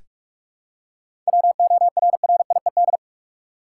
Morse code, heard as a keyed tone, about 46 wpm, 700 Hz. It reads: WORRIED
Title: Seven Letter Words - 45wpm